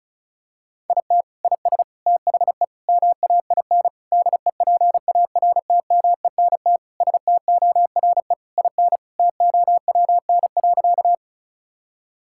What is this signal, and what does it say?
Morse code, heard as a keyed tone, about 36 wpm, 700 Hz. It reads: IT IS THE MAIN DEPARTMENT STORE IN TOWN.
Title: Sentences from Top 1000 Words - Review Part 5 - 35wpm